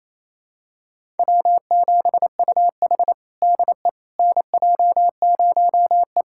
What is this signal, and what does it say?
Morse code, heard as a keyed tone, about 28 wpm, 700 Hz. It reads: W7UH DE NJ0E
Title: USA Callsign DE Callsign - 4 Characters - 28wpm